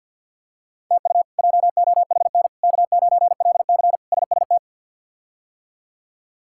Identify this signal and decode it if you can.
Morse code, heard as a keyed tone, about 53 wpm, 700 Hz. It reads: TU JOHN K9LC SST
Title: K1USN SST QSO Practice - 50wpm